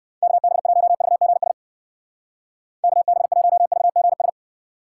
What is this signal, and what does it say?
Morse code, heard as a keyed tone, about 58 wpm, 700 Hz. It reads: KB1VGS KB1VGS